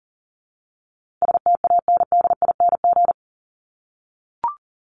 Morse code, heard as a keyed tone, about 39 wpm, 700 Hz. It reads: STANDING
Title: ING Suffix - Top 100 Words with Base Words 40wpm